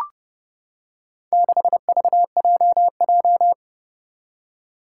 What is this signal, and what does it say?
Morse code, heard as a keyed tone, about 29 wpm, 690 Hz. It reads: 6VJJ
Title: Four Challenging Letter-Number Combinations 30wpm